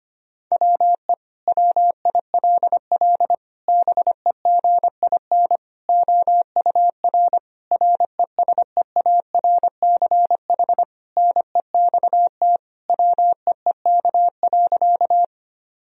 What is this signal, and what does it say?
Morse code, heard as a keyed tone, about 25 wpm, 700 Hz. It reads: WE WILL BEGIN OUR RESEARCH NEXT WEEK.
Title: Sentences from Top 1000 Words - Review Part 4 - 25wpm